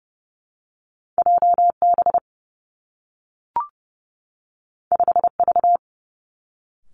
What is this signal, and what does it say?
Morse code, heard as a keyed tone, about 30 wpm, 700 Hz. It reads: J B 5 V
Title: Two Challenging Letter-Number Combinations 30wpm